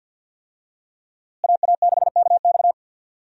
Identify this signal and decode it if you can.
Morse code, heard as a keyed tone, about 51 wpm, 700 Hz. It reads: AA6KX
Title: USA Call Signs - 5 Characters - Repeated 2x - 50wpm